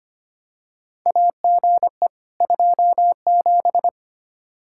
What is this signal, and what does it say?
Morse code, heard as a keyed tone, about 25 wpm, 700 Hz. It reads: AGE 27
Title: K1USN SST QSO Practice 25wpm